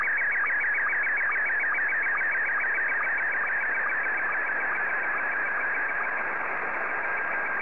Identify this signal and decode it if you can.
SSTV (slow-scan television), Scottie 1.